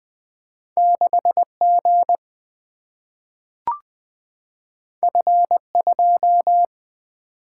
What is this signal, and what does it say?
Morse code, heard as a keyed tone, about 20 wpm, 700 Hz. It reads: 6G F2
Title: Single-Letter Single-Number 20wpm